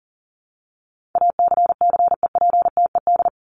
Morse code, heard as a keyed tone, about 40 wpm, 700 Hz. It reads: ACCEPTED
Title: ED Suffix - Top 100 Words with Base Words 40wpm